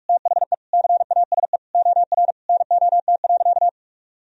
Morse code, heard as a keyed tone, about 44 wpm, 700 Hz. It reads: THE CASE OR NOT.
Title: Sentences from Top 1000 Words - Review Part 3 - 45wpm